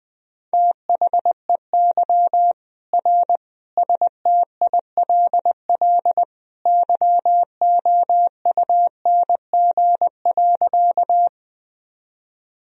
Morse code, heard as a keyed tone, about 21 wpm, 700 Hz. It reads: THEY R STILL YOUNG.
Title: Sentences from Top 400 Words and Abbreviations 20wpm